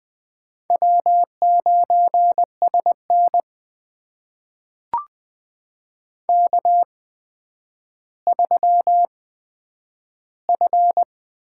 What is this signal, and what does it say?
Morse code, heard as a keyed tone, about 20 wpm, spaced as for 10 wpm, 700 Hz. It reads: W9SN K3F
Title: USA Call Signs - 4 Characters - 8x Character Spacing - 20wpm